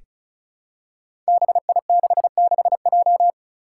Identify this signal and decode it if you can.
Morse code, heard as a keyed tone, about 34 wpm, 700 Hz. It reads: BI66J
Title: Five Challenging Letter-Number Combinations 35wpm